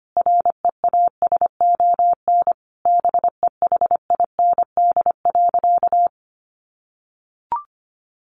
Morse code, heard as a keyed tone, about 25 wpm, 700 Hz. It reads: REASON BEHIND.
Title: Sentences from Top 700 Words 25wpm